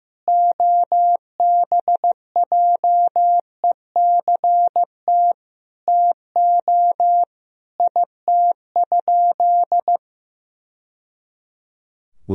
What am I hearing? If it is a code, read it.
Morse code, heard as a keyed tone, about 16 wpm, 700 Hz. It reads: OBJECT TO IT?
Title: Sentences from Top 1000 Words - Review Part 3 - 15wpm